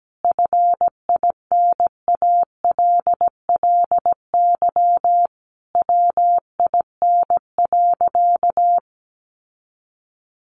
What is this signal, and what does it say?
Morse code, heard as a keyed tone, about 17 wpm, 700 Hz. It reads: FINALLY WIN.
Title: Sentences from Top 1500 Words 17wpm